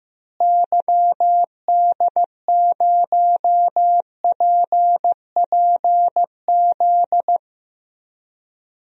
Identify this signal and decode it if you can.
Morse code, heard as a keyed tone, about 15 wpm, 700 Hz. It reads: YD0PPZ
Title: DXCC-100 Call Signs - 6 Characters - Repeated 3x - 15wpm